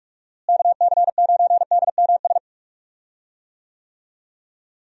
Morse code, heard as a keyed tone, about 43 wpm, 700 Hz. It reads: KC9DMS
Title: USA Call Signs - 6 Characters - Repeated 3x - 45wpm